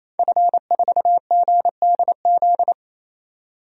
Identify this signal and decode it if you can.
Morse code, heard as a keyed tone, about 28 wpm, 700 Hz. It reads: F4GDZ
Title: DXCC-100 Call Signs - 5 Characters - 28wpm